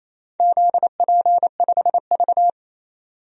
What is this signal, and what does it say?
Morse code, heard as a keyed tone, about 28 wpm, 700 Hz. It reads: ZP5V
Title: DXCC-100 Call Signs - 4 Characters - 28wpm